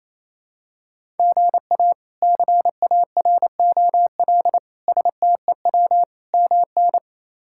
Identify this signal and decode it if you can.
Morse code, heard as a keyed tone, about 28 wpm, 700 Hz. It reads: GA CAROL STEW MN
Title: K1USN SST QSO Practice - 28wpm